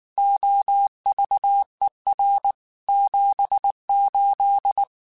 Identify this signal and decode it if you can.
Morse code, heard as a keyed tone, about 19 wpm, 800 Hz. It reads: OVER 78
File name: QSOs / QSOs-mp3s / WA7QOX 2@18